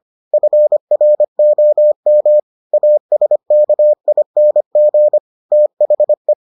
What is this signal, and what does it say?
Morse code, heard as a keyed tone, about 25 wpm, 600 Hz. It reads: FROM ASKING THE